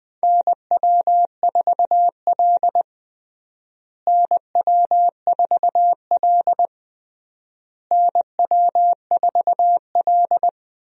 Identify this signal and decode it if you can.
Morse code, heard as a keyed tone, about 20 wpm, 700 Hz. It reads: NW4L NW4L NW4L
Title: USA Call Signs - 4 Characters - Repeated 3x - 20wpm